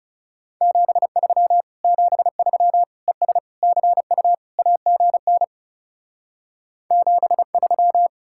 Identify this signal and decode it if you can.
Morse code, heard as a keyed tone, about 36 wpm, 700 Hz. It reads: 73 73 ES CU AGN 73
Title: Speed-Racing Common QSO Phrases Encore - 30wpm